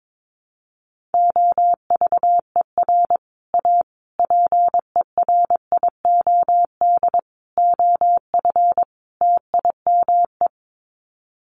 Morse code, heard as a keyed tone, about 22 wpm, 700 Hz. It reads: OVER A PERIOD OF TIME
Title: Sets of 5 Words Encore2 22wpm